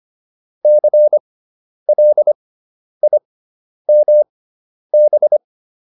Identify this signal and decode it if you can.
Morse code, heard as a keyed tone, about 25 wpm, spaced as for 10 wpm, 600 Hz. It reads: CLIMB